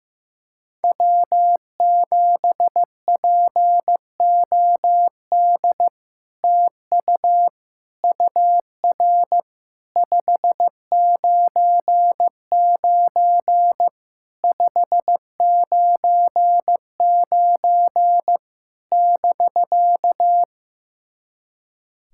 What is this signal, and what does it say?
Morse code, heard as a keyed tone, about 15 wpm, 700 Hz. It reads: W7POD TU UR 599 599 <BK>
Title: SOTA QSO Practice 15wpm